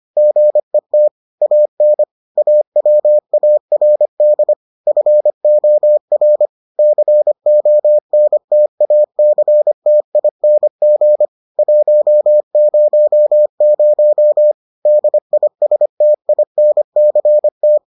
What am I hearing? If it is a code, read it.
Morse code, heard as a keyed tone, about 25 wpm, 600 Hz. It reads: GET AN AWARD FOR CONTACTING 100 DISTINCT